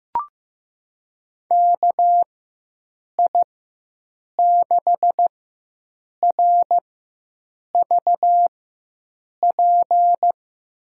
Morse code, heard as a keyed tone, about 16 wpm, spaced as for 7 wpm, 700 Hz. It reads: KI6RVP
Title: USA Call Signs - 6 Characters - 4x Character Spacing - 15wpm